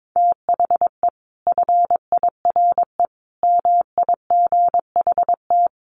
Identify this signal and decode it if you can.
Morse code, heard as a keyed tone, about 22 wpm, 700 Hz. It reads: THE FIRE MIGHT
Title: Sentences from Top 1300 Words 22wpm